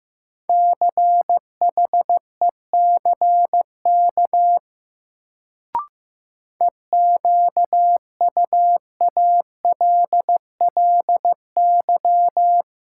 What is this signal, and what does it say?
Morse code, heard as a keyed tone, about 15 wpm, 700 Hz. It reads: CHECK EQUALLY
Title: Sentences from Top 1400 Words - New Word Drill Alt - 15wpm